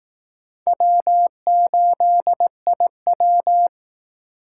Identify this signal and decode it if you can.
Morse code, heard as a keyed tone, about 17 wpm, 700 Hz. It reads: W8IW
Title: Speed Racing - USA 1x2 Call Signs 15wpm